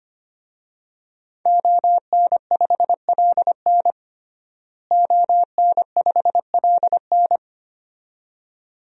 Morse code, heard as a keyed tone, about 25 wpm, 700 Hz. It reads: ON5LN ON5LN
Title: DXCC-100 Call Signs - 5 Characters - Repeated 2x - 25wpm